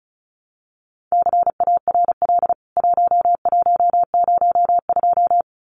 Morse code, heard as a keyed tone, about 34 wpm, 700 Hz. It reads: CARL 1102
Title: CWT Exchange 35wpm